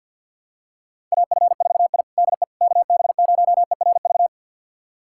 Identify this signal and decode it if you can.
Morse code, heard as a keyed tone, about 50 wpm, 700 Hz. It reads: AF4I DE KB0ERV
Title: USA Callsign DE Callsign 50wpm